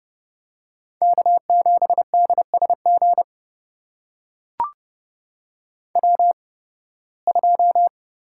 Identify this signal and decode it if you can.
Morse code, heard as a keyed tone, about 32 wpm, spaced as for 20 wpm, 700 Hz. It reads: K7DSG W2
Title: USA Call Signs - 5 Characters - 8x Character Spacing - 30wpm